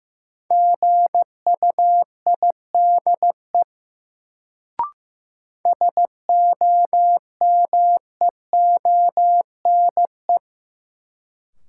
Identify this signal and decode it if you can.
Morse code, heard as a keyed tone, about 15 wpm, 700 Hz. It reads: GUIDE SOMEONE
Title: Sentences from Top 600 Words - New Word Drill - 15wpm